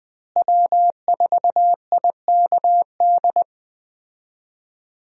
Morse code, heard as a keyed tone, about 21 wpm, 700 Hz. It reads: W4IKD
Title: USA Call Signs - 5 Characters - Repeated 2x - 20wpm